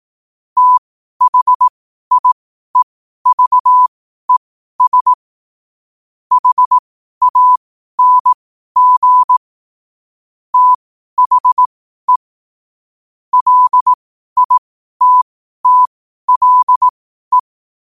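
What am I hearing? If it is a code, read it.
Morse code, heard as a keyed tone, about 18 wpm, spaced as for 12 wpm, 1000 Hz. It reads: THIEVES HANG THE LITTLE
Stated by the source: Morse question-of-the-day sent at 12 wpm.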